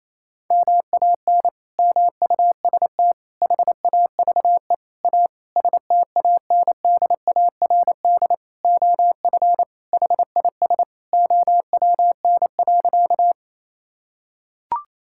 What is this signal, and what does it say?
Morse code, heard as a keyed tone, about 28 wpm, 700 Hz. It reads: MAN MUST HAVE A STANDARD OF HIS OWN.
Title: Sentences from Top 1000 Words - Review Part 6 - 28wpm